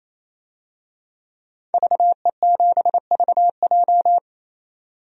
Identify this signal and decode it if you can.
Morse code, heard as a keyed tone, about 28 wpm, 700 Hz. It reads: VE7VJ
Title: DXCC-100 Call Signs - 5 Characters - 28wpm